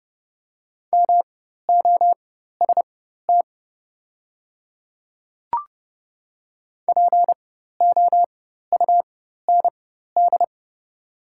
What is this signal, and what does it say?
Morse code, heard as a keyed tone, about 29 wpm, spaced as for 14 wpm, 690 Hz. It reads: MOST POUND
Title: 4x Character Spacing - Top 500 Words - 30wpm